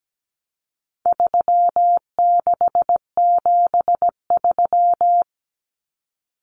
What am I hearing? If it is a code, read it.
Morse code, heard as a keyed tone, about 17 wpm, 700 Hz. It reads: 3673
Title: Common Cut Numbers - 4-Digit 17wpm